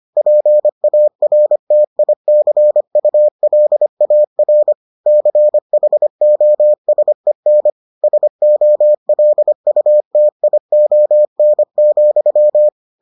Morse code, heard as a keyed tone, about 25 wpm, 600 Hz. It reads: PARTICULAR CHOSEN SOLUTION,